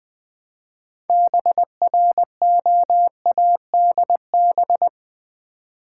Morse code, heard as a keyed tone, about 20 wpm, 700 Hz. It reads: BROADB
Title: Build A Word - 3 Letters At A Time - 9-Letter Words - 20wpm